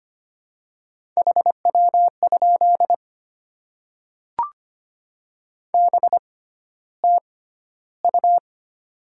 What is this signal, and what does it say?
Morse code, heard as a keyed tone, about 25 wpm, spaced as for 13 wpm, 700 Hz. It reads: HW? BTU
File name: Top 20 QSO Elements - 6x Character Spacing - 25wpm